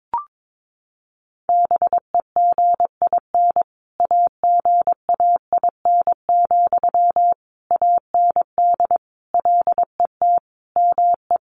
Morse code, heard as a keyed tone, about 22 wpm, 700 Hz. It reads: BEGIN AGAIN, AND LET ME
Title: Sentences from Top 800 Words 22wpm